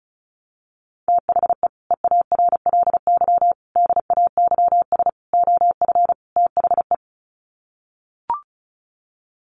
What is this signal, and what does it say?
Morse code, heard as a keyed tone, about 34 wpm, 700 Hz. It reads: THE EARLY DAYS OF THE
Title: Sets of 5 Words Encore 35wpm